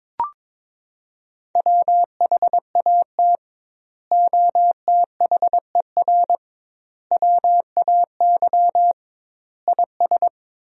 Morse code, heard as a keyed tone, about 22 wpm, 700 Hz. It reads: WHAT OTHER WAY IS
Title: 2x Word-Spacing - Sentences from Top 100 Words - 22wpm